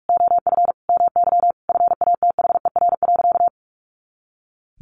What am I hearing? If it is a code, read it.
Morse code, heard as a keyed tone, about 44 wpm, 700 Hz. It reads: OF MY FATHER.
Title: Sentences from Top 500 Words 45wpm